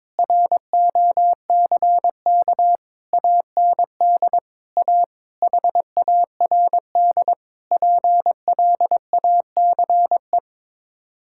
Morse code, heard as a keyed tone, about 22 wpm, 700 Hz. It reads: ROCK AND A HARD PLACE
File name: practice / 22wpm / Sets of 7 Words - 22wpm